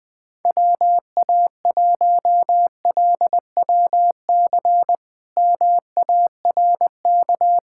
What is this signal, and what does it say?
Morse code, heard as a keyed tone, about 20 wpm, 700 Hz. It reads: WA1LWC MARK
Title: K1USN SST QSO Practice - 20wpm